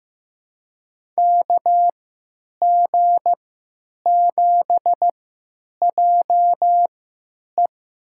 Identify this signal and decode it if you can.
Morse code, heard as a keyed tone, about 16 wpm, spaced as for 9 wpm, 700 Hz. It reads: KG7JE
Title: USA Call Signs - 3x Character Spacing - 15wpm